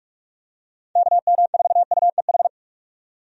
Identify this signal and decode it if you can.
Morse code, heard as a keyed tone, about 45 wpm, 700 Hz. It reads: KM4UEH
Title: USA Call Signs - 6 Characters - 45wpm